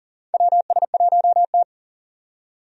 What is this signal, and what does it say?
Morse code, heard as a keyed tone, about 38 wpm, 700 Hz. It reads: WS1T
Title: USA Call Signs - 4 Characters - 40wpm